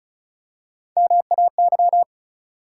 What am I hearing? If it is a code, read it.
Morse code, heard as a keyed tone, about 34 wpm, 700 Hz. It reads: MAY
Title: Three Letter Words - 35wpm